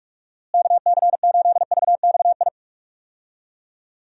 Morse code, heard as a keyed tone, about 42 wpm, 700 Hz. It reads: KC8VXI
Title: USA Call Signs - 6 Characters - Repeated 3x - 45wpm